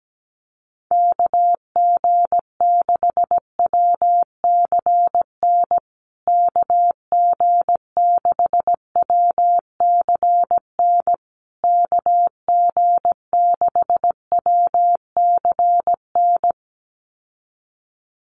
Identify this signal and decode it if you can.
Morse code, heard as a keyed tone, about 17 wpm, 700 Hz. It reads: KG6WCN KG6WCN KG6WCN
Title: USA Call Signs Repeated 3x 17wpm